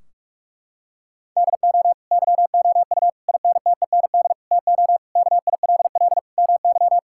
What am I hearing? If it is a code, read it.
Morse code, heard as a keyed tone, about 41 wpm, 700 Hz. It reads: DO YOU INTEND TO KILL MY
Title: Sentences from Top 1600 Words 45wpm